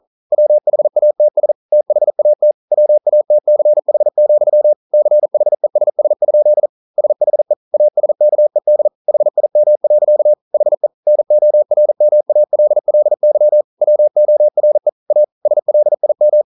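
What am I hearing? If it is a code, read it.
Morse code, heard as a keyed tone, about 42 wpm, 600 Hz. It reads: WHATS THAT WATCH, CHESS? SHE ASKED HIM. HE NORMALLY WORE A SLIM
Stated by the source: Morse code practice at 41 wpm, from CW med Gnister.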